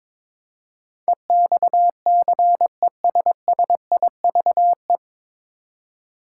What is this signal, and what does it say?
Morse code, heard as a keyed tone, about 22 wpm, 700 Hz. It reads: EXCESSIVE
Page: Morse Code Ninja practice 22wpm